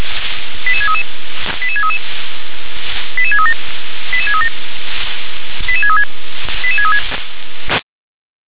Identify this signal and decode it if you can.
Five-tone selective call (ZVEI), heard as quick runs of five tones, tones of about 70 ms. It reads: 88633 88633 88637 88637 88636 88636